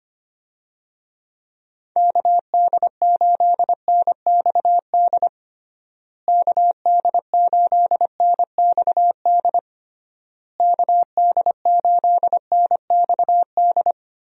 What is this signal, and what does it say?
Morse code, heard as a keyed tone, about 25 wpm, 700 Hz. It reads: KD8NXD KD8NXD KD8NXD
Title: USA Call Signs - 6 Characters - Repeated 3x - 25wpm